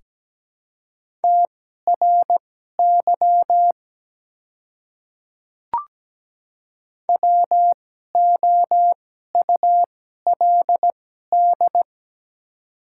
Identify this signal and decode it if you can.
Morse code, heard as a keyed tone, about 17 wpm, spaced as for 12 wpm, 700 Hz. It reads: TRY WOULD
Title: 2x Character Spacing - Top 300 Words - 17wpm